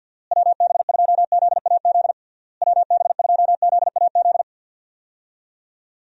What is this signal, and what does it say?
Morse code, heard as a keyed tone, about 50 wpm, 700 Hz. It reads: WB2ZAZ WB2ZAZ
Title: USA Call Signs - 6 Characters - Repeated 3x - 50wpm